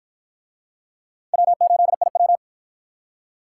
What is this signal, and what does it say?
Morse code, heard as a keyed tone, about 53 wpm, 700 Hz. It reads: W8IW